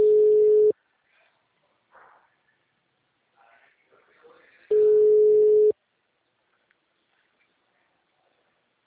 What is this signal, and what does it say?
Ringback tone (Europe, 425 Hz), with speech.
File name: out- -2035- - - .4